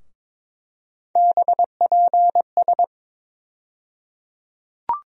Morse code, heard as a keyed tone, about 22 wpm, 700 Hz. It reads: BPS
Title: USA Call Sign Postfixes - 3 Letters 22wpm